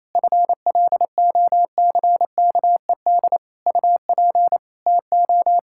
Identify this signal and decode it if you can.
Morse code, heard as a keyed tone, about 28 wpm, 700 Hz. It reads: FLOCKED UP TO
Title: Pangram Challenge v2 - 28wpm